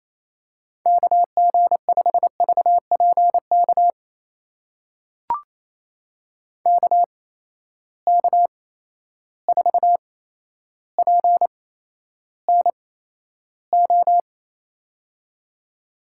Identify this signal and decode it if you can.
Morse code, heard as a keyed tone, about 28 wpm, spaced as for 12 wpm, 700 Hz. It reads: KG5VPK KK4PNO
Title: USA Call Signs - 6 Characters - 8x Character Spacing - 28wpm